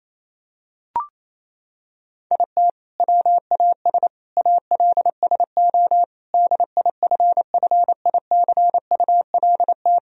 Morse code, heard as a keyed tone, about 28 wpm, 700 Hz. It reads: IT WAS ALSO DIFFICULT
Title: Sentences from Top 1000 Words - Review Part 1 - 28wpm